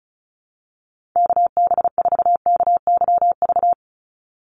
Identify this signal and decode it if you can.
Morse code, heard as a keyed tone, about 34 wpm, 700 Hz. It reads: KB4KYV
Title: USA Call Signs 35wpm